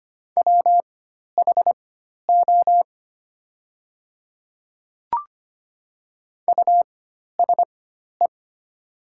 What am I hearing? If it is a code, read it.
Morse code, heard as a keyed tone, about 25 wpm, spaced as for 12 wpm, 700 Hz. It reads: WHO USE
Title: 4x Character Spacing - Top 200 Words - 25wpm